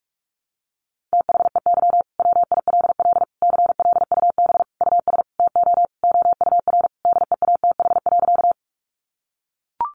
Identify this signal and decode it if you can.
Morse code, heard as a keyed tone, about 44 wpm, 700 Hz. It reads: THEY WILL CLUB US TO OUR DEATH.
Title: Sentences from Top 1200 Words 45wpm